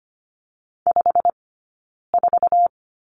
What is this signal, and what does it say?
Morse code, heard as a keyed tone, about 25 wpm, 700 Hz. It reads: 5 4
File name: course / 25:10wpm / 092 Lesson 16d - Letter c - Two Characters 25wpm